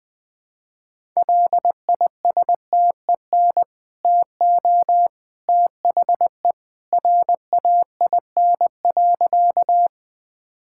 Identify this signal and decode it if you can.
Morse code, heard as a keyed tone, about 20 wpm, 700 Hz. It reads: LISTEN TO THE RAIN.
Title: Easy Sentences with Hints - 20wpm